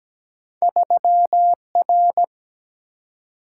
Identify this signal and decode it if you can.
Morse code, heard as a keyed tone, about 17 wpm, 700 Hz. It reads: 3R
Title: Single-Letter Single-Number 17wpm